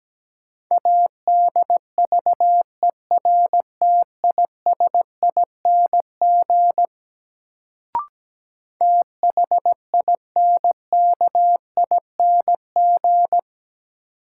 Morse code, heard as a keyed tone, about 17 wpm, 700 Hz. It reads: ADVERTISING THINKING
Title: Sentences from Top 1000 Words - Review Part 6 - New Word Drill - 17wpm